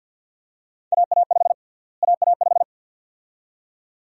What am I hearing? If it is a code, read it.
Morse code, heard as a keyed tone, about 48 wpm, 700 Hz. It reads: AA5 AA5
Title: USA Call Sign Prefixes - Repeated 2x - 50wpm